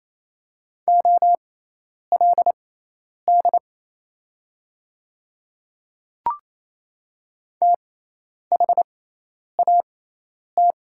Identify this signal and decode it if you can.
Morse code, heard as a keyed tone, about 27 wpm, spaced as for 10 wpm, 690 Hz. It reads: OLD THAT
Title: 6x Character Spacing - Top 300 Words - 28wpm